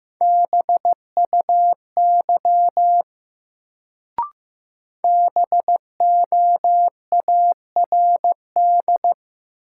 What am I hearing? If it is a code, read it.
Morse code, heard as a keyed tone, about 15 wpm, 700 Hz. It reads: BUY BOARD
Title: Sentences from Top 700 Words - New Word Drill - 15wpm